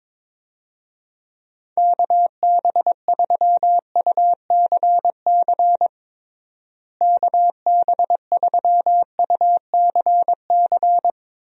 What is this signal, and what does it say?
Morse code, heard as a keyed tone, about 22 wpm, 700 Hz. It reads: KB3UCC KB3UCC
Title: USA Call Signs - 6 Characters - Repeated 3x - 22wpm